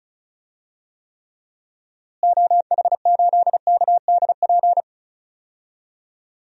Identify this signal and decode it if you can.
Morse code, heard as a keyed tone, about 34 wpm, 700 Hz. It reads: OH8KDP
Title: DXCC-100 Call Signs - 6 Characters - Repeated 2x - 35wpm